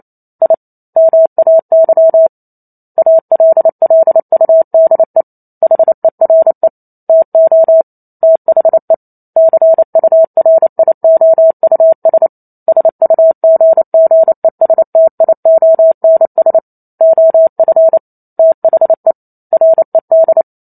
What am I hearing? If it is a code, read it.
Morse code, heard as a keyed tone, about 28 wpm, 650 Hz. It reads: I MAY ALLUDE HERE TO THE CURIOUS SUGGESTIONS OF THE RED